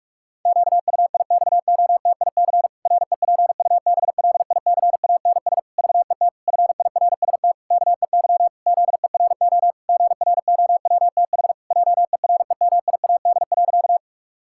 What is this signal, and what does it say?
Morse code, heard as a keyed tone, about 46 wpm, 700 Hz. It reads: QUIXOTIC REPUBLICANS VET FIRST KEY ZERO GROWTH JEREMIAD.
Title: Pangram Challenge v2 - 45wpm